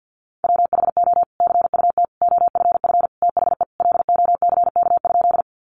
Morse code, heard as a keyed tone, about 50 wpm, 700 Hz. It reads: WHO CUT OFF THE LOCK?
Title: Sentences from Top 1300 Words 50wpm